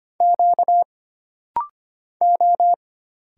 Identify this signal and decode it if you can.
Morse code, heard as a keyed tone, about 25 wpm, 700 Hz. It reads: Q O
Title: Single Letters - Rapid-Fire 25wpm